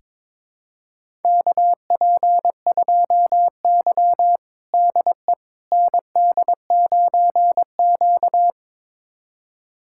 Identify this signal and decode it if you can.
Morse code, heard as a keyed tone, about 22 wpm, 700 Hz. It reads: KP2Y DE ND9Q
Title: USA Callsign DE Callsign - 4 Characters - 22wpm